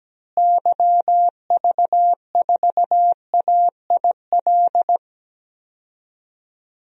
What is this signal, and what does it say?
Morse code, heard as a keyed tone, about 17 wpm, 700 Hz. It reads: YV4AIL
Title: DXCC-100 Call Signs - 6 Characters - Repeated 2x - 17wpm